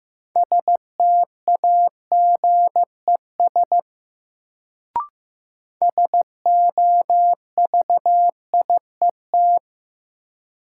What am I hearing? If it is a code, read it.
Morse code, heard as a keyed tone, about 15 wpm, 700 Hz. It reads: STAGES SOVIET
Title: Sets of 5 Words Encore - New Word Drill - 15wpm